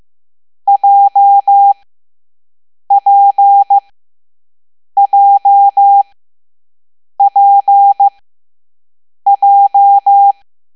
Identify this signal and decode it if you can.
Morse code, heard as a keyed tone, about 15 wpm, spaced as for 6 wpm, 800 Hz. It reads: JPJPJ
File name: Morse / Level7B